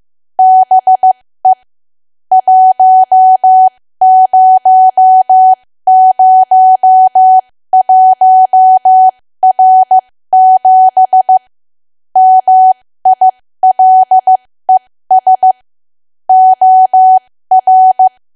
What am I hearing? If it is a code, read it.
Morse code, heard as a keyed tone, about 15 wpm, spaced as for 13 wpm, 750 Hz. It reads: BE 1001R7 MILES OR
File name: cwa / W1AW-Level2 / 13WPM